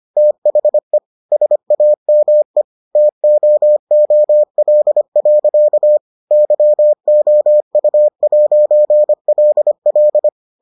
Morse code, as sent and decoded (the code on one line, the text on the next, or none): - .... / ... .- -- . / - --- --- .-.. .-.-.- / -.-- --- ..- .----. .-.. .-..
THE SAME TOOL. YOU'LL